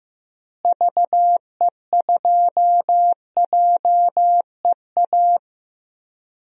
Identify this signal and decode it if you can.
Morse code, heard as a keyed tone, about 15 wpm, 700 Hz. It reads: VE2JEA
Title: DXCC-100 Call Signs - 6 Characters - Repeated 2x - 15wpm